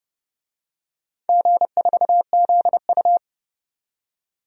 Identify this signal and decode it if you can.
Morse code, heard as a keyed tone, about 32 wpm, 700 Hz. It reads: G4ZU
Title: DXCC-100 Call Signs - 4 Characters - 30wpm